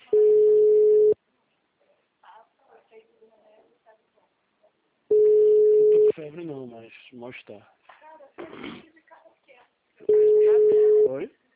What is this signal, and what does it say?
Ringback tone (Europe, 425 Hz), with speech.
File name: out- -2024- - - .77876